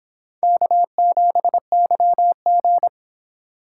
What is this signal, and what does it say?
Morse code, heard as a keyed tone, about 26 wpm, 700 Hz. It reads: K7YG